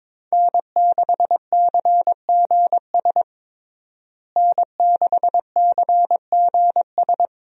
Morse code, heard as a keyed tone, about 22 wpm, 700 Hz. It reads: N6CGS N6CGS
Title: USA Call Signs - 5 Characters - Repeated 3x - 22wpm